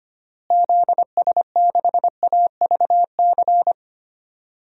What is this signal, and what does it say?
Morse code, heard as a keyed tone, about 25 wpm, 700 Hz. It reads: ZS6AVC
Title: DXCC-100 Call Signs - 6 Characters - Repeated 3x - 25wpm